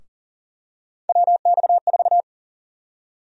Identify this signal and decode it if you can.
Morse code, heard as a keyed tone, about 40 wpm, 700 Hz. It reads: WX4
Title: USA Call Sign Prefixes - 40wpm